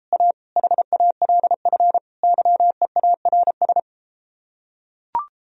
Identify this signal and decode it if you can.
Morse code, heard as a keyed tone, about 34 wpm, 700 Hz. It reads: A HALF YEARS